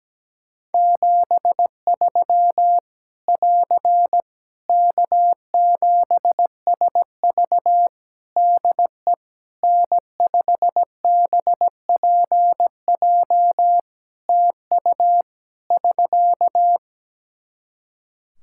Morse code, heard as a keyed tone, about 17 wpm, 700 Hz. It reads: 73 <AR> K7SV DE N5BPJ TU <SK>
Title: Practice QSO - Exchange 4 - Ending - 17wpm